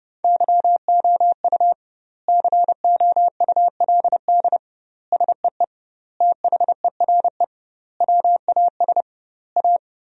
Morse code, heard as a keyed tone, about 30 wpm, 700 Hz. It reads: YOU COULD SEE THERE WAS A
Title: 2x Word-Spacing - Sentences from Top 500 Words - 30wpm